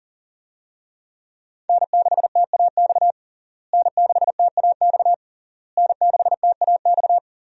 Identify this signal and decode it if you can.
Morse code, heard as a keyed tone, about 40 wpm, 700 Hz. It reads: N6TAX N6TAX N6TAX
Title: USA Call Signs - 5 Characters - Repeated 3x - 40wpm